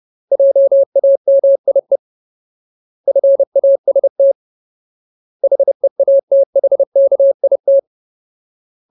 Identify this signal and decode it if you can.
Morse code, heard as a keyed tone, about 30 wpm, 550 Hz. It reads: JAMIE FAST HEATHKIT
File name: Ultimate Word List / 30WPM Ultimate